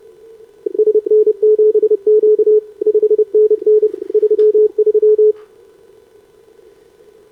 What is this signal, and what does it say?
Morse code, heard as a keyed tone, about 30 wpm, 440 Hz. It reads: SN7Q 5NN 33
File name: CQWWCW2025 / SobRanCN3A 10m